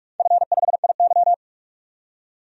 Morse code, heard as a keyed tone, about 42 wpm, 700 Hz. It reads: F5IY
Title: DXCC-100 Call Signs - 4 Characters - 45wpm